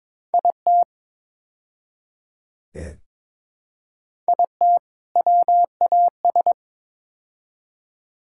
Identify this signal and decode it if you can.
Morse code, heard as a keyed tone, about 22 wpm, 700 Hz. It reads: IT IT WAS
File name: practice / 22wpm / Build A Phrase - 5-Word Phrases - 22wpm